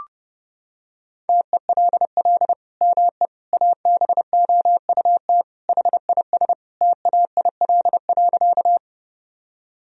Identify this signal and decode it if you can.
Morse code, heard as a keyed tone, about 32 wpm, 700 Hz. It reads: TELL ME ABOUT HIS TAIL.
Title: Sentences from Top 1000 Words - Review Part 3 - 30wpm